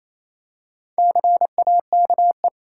Morse code, heard as a keyed tone, about 28 wpm, 700 Hz. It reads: CAKE
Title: Four Letter Words - 28wpm